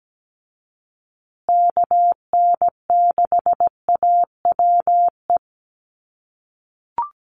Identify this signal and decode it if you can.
Morse code, heard as a keyed tone, about 17 wpm, 700 Hz. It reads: KN6AWE
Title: USA Call Signs Repeated 3x 17wpm